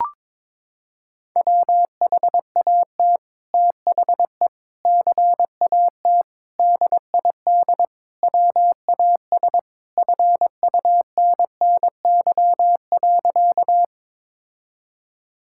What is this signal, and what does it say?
Morse code, heard as a keyed tone, about 22 wpm, 700 Hz. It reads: WHAT THE CAT DID WAS FUNNY.
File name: practice / 22wpm / Sentences with 2-Word Phrases - 22wpm